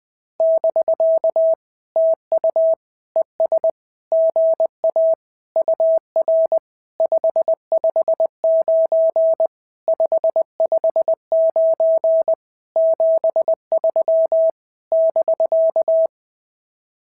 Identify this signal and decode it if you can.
Morse code, heard as a keyed tone, about 19 wpm, 650 Hz. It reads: <BK> TU ES GA UR 559 559 73 <BK>
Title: SOTA QSO Practice 20wpm